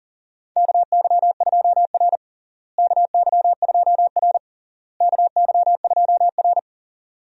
Morse code, heard as a keyed tone, about 40 wpm, 700 Hz. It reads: KY2R KY2R KY2R
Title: USA Call Signs - 4 Characters - Repeated 3x - 40wpm